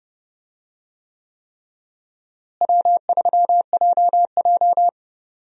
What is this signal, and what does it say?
Morse code, heard as a keyed tone, about 31 wpm, 700 Hz. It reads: W3JJ
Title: USA Call Signs - 4 Characters - Repeated 3x - 30wpm